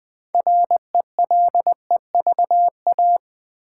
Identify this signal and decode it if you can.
Morse code, heard as a keyed tone, about 20 wpm, 700 Hz. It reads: RELEVA
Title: Build A Word - 3 Letters At A Time - 8-Letter Words - 20wpm